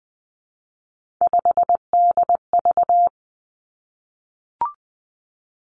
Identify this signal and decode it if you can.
Morse code, heard as a keyed tone, about 20 wpm, 700 Hz. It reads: 5DV
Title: Three Challenging Letter-Number Combinations 20wpm